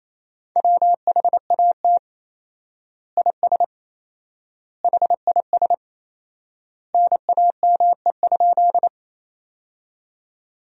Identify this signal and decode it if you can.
Morse code, heard as a keyed tone, about 28 wpm, 700 Hz. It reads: WHAT IS HIS NAME?
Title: 4x Word-Spacing - Sentences from Top 200 Words - 28wpm